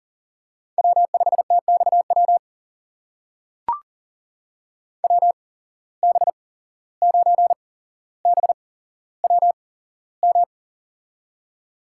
Morse code, heard as a keyed tone, about 40 wpm, spaced as for 16 wpm, 700 Hz. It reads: W5TXW WB9BWM
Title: USA Call Signs - 8x Character Spacing - 40wpm